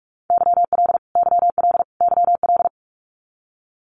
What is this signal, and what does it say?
Morse code, heard as a keyed tone, about 44 wpm, 700 Hz. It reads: YL YL YL
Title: Abbreviations Repeated 3x - 45wpm